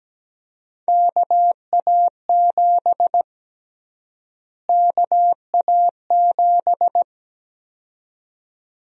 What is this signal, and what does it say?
Morse code, heard as a keyed tone, about 17 wpm, 700 Hz. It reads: KA7 KA7
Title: USA Call Sign Prefixes - Repeated 2x - 17wpm